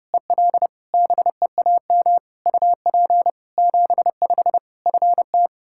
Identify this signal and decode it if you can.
Morse code, heard as a keyed tone, about 30 wpm, 700 Hz. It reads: EL BEAM UP 75 FT
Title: Practice QSO - Exchange 2 - Info - 30wpm